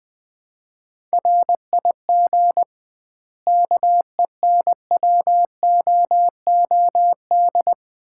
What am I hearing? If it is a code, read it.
Morse code, heard as a keyed tone, about 21 wpm, 700 Hz. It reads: RIG KENWOOD
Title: Common QSO Phrases - 2x Word-Spacing - 20wpm